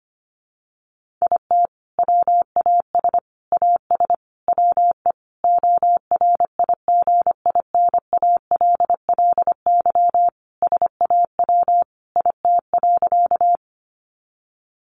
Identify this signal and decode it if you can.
Morse code, heard as a keyed tone, about 25 wpm, 700 Hz. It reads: IT WAS AS WE ORIGINALLY SAW IT.
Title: Sentences from Top 1500 Words 25wpm